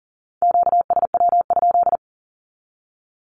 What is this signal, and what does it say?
Morse code, heard as a keyed tone, about 40 wpm, 700 Hz. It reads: QSW?
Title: Q-Codes with Spoken Meaning - 40wpm